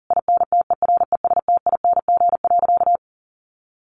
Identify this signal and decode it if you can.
Morse code, heard as a keyed tone, about 37 wpm, 700 Hz. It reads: INTERESTING.
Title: Sentences from Top 1200 Words 40wpm